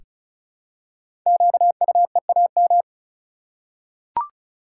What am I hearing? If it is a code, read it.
Morse code, heard as a keyed tone, about 36 wpm, 700 Hz. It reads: QUEAM
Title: Code Groups - 5 Letters - 35wpm